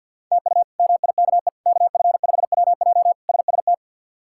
Morse code, heard as a keyed tone, about 51 wpm, 700 Hz. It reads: TU MIKE KF5WJ SST
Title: K1USN SST QSO Practice - 50wpm